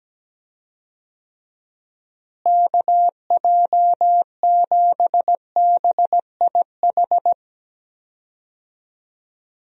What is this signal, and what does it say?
Morse code, heard as a keyed tone, about 17 wpm, 700 Hz. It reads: KJ7BIH
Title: USA Call Signs - 6 Characters - Repeated 3x - 17wpm